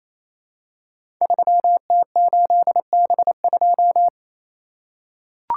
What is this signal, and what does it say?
Morse code, heard as a keyed tone, about 28 wpm, 700 Hz. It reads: 3T8B2
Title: Code Groups - 5 Letter:Numbers - 28wpm